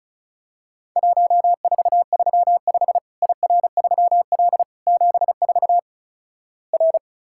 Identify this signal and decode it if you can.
Morse code, heard as a keyed tone, about 36 wpm, 700 Hz. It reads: 1435 IR3L 74 R
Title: Worked All Europe DX Contest - QTC Exchange - 35wpm